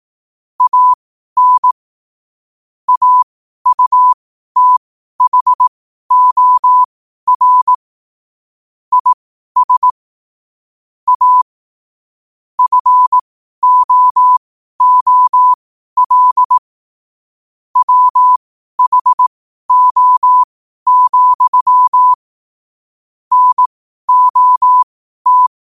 Morse code, heard as a keyed tone, about 18 wpm, spaced as for 12 wpm, 1000 Hz. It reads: AN AUTHOR IS A FOOL WHO, NOT
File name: qotdfiles / 12wpm